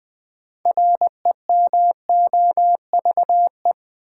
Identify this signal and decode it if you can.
Morse code, heard as a keyed tone, about 20 wpm, 700 Hz. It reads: REMOVE
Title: Sentences from Top 800 Words - New Word Drill - 20wpm